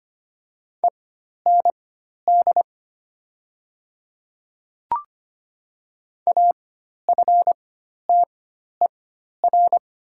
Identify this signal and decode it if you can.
Morse code, heard as a keyed tone, about 25 wpm, spaced as for 12 wpm, 700 Hz. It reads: END AFTER
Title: 4x Character Spacing - Top 500 Words - 25wpm